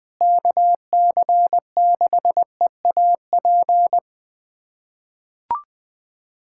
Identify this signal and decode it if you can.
Morse code, heard as a keyed tone, about 20 wpm, 700 Hz. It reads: KC6EAP
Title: USA Callsign DE Callsign 20wpm